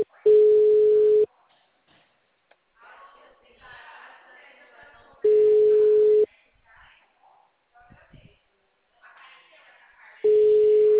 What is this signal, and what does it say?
Ringback tone (Europe, 425 Hz), with speech.
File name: out- -2044- - - .17379